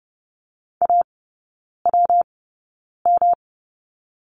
Morse code, heard as a keyed tone, about 30 wpm, spaced as for 9 wpm, 700 Hz. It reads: AWM
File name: Lesson 19e - Letter w - Three Characters 30wpm